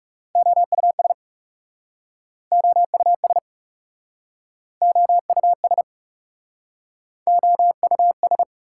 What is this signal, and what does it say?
Morse code, heard as a keyed tone, about 40 wpm, 700 Hz. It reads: OUS OUS OUS OUS